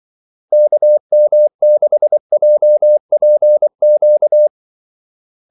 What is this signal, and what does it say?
Morse code, heard as a keyed tone, about 24 wpm, 600 Hz. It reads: KM6JPQ
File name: Callsigns-24wpm